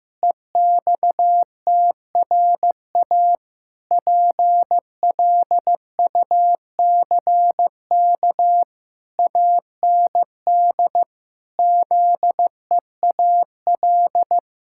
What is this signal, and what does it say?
Morse code, heard as a keyed tone, about 15 wpm, 700 Hz. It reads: EXTRA PLUCK AND ZEAL
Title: Pangram Challenge v2 - 15wpm